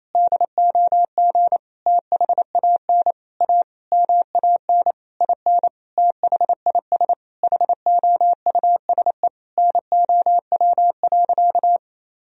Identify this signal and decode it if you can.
Morse code, heard as a keyed tone, about 27 wpm, 700 Hz. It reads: DOG THAN A MAN IN THIS HOUSE NOW.
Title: Sentences from Top 1000 Words - Review Part 2 - 28wpm